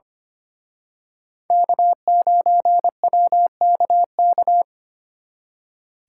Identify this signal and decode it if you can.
Morse code, heard as a keyed tone, about 29 wpm, 700 Hz. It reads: K9WKK
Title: Speed Racing - USA Call Signs 25wpm